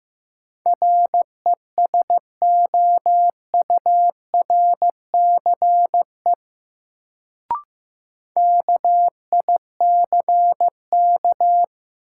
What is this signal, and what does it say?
Morse code, heard as a keyed tone, about 15 wpm, 700 Hz. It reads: RESOURCE KICK
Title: Sentences from Top 1300 Words - New Word Drill Alt - 15wpm